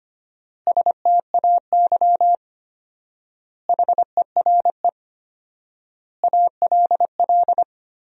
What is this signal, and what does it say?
Morse code, heard as a keyed tone, about 25 wpm, 700 Hz. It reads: STAY HERE ALL
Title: 4x Word-Spacing - Sentences from Top 500 Words - 25wpm